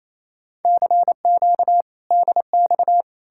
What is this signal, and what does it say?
Morse code, heard as a keyed tone, about 37 wpm, 700 Hz. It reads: CQ DX